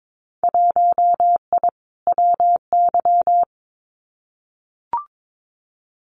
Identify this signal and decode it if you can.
Morse code, heard as a keyed tone, about 22 wpm, 700 Hz. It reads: 1I WY
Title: Winter Field Day Exchanges Encore 22wpm